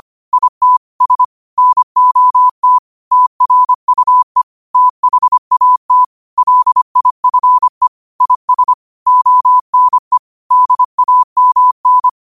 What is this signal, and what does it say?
Morse code, heard as a keyed tone, about 25 wpm, 1000 Hz. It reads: IT S NOT TRUE THAT LIFE IS ONE DAMN
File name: qotdfiles / 25wpm